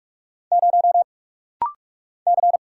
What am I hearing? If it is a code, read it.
Morse code, heard as a keyed tone, about 44 wpm, 700 Hz. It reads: 0 C
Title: Single Letter-Number Alt - Rapid-Fire 45wpm